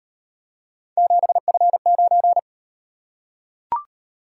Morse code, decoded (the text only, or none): 7F9